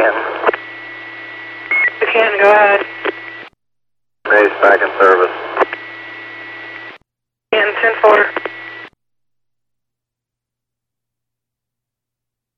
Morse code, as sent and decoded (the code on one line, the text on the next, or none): none